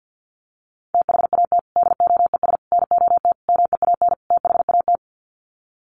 Morse code, heard as a keyed tone, about 50 wpm, 700 Hz. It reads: THAT DOES NOT MEAN THAT
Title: Sets of 5 Words Encore 50wpm